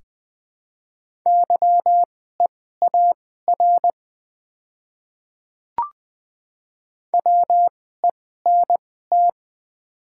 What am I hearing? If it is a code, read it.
Morse code, heard as a keyed tone, about 21 wpm, spaced as for 15 wpm, 700 Hz. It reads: YEAR WENT
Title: 2x Character Spacing - Top 300 Words - 20wpm